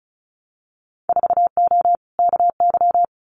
Morse code, heard as a keyed tone, about 34 wpm, 700 Hz. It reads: 4O KY
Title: Winter Field Day Exchanges Encore 35wpm